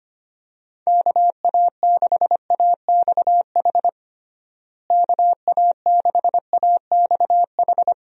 Morse code, heard as a keyed tone, about 25 wpm, 700 Hz. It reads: KA6AXH KA6AXH
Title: USA Call Signs - 6 Characters - Repeated 3x - 25wpm